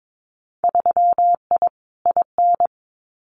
Morse code, heard as a keyed tone, about 22 wpm, 700 Hz. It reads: 3I IN
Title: Winter Field Day Exchanges 22wpm